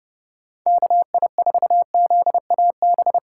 Morse code, heard as a keyed tone, about 32 wpm, 700 Hz. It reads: KI4ZAB